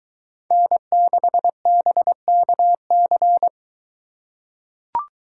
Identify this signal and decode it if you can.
Morse code, heard as a keyed tone, about 22 wpm, 700 Hz. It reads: N6BKC